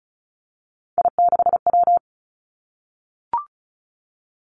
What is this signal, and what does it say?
Morse code, heard as a keyed tone, about 36 wpm, 700 Hz. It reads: I6W
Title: Three Challenging Letter-Number Combinations 35wpm